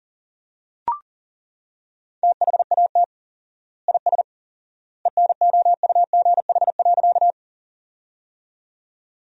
Morse code, heard as a keyed tone, about 37 wpm, 700 Hz. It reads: THAT IS ENOUGH.
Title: 4x Word-Spacing - Sentences from Top 400 Words - 40wpm